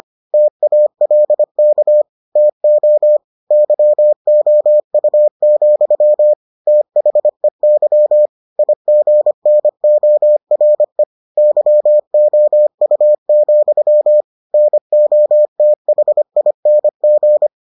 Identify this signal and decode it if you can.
Morse code, heard as a keyed tone, about 25 wpm, 600 Hz. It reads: TALK TO YOU, THEY IGNORE YOU, NOTHING